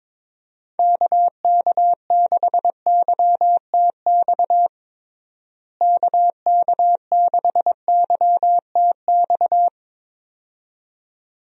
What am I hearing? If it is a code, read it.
Morse code, heard as a keyed tone, about 21 wpm, 700 Hz. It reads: KK6YTX KK6YTX
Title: USA Call Signs - 6 Characters - Repeated 2x - 22wpm